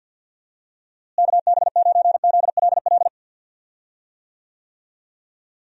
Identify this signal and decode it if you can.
Morse code, heard as a keyed tone, about 52 wpm, 700 Hz. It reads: KB9ZLL
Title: USA Call Signs - 6 Characters - Repeated 3x - 50wpm